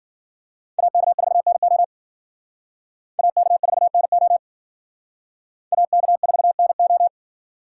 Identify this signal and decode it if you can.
Morse code, heard as a keyed tone, about 50 wpm, 700 Hz. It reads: AK4NO AK4NO AK4NO
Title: Speed Racing - USA 2x2 Call Signs 40wpm